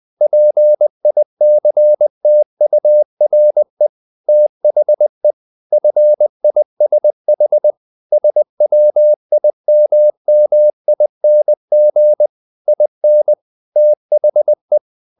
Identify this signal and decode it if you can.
Morse code, heard as a keyed tone, about 20 wpm, 600 Hz. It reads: PICTURE THE FISH SWIMMING IN THE
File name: the old man and the sea / the old man and the sea part 1 20wpm